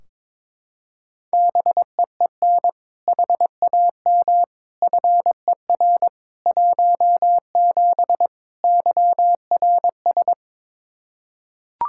Morse code, heard as a keyed tone, about 22 wpm, 700 Hz. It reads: BEEN HAM FER 17 YRS
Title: Practice QSO - Exchange 3 - More Info - 22wpm